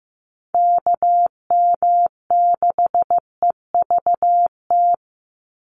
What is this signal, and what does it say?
Morse code, heard as a keyed tone, about 15 wpm, 700 Hz. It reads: KM6EVT
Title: USA Call Signs 15wpm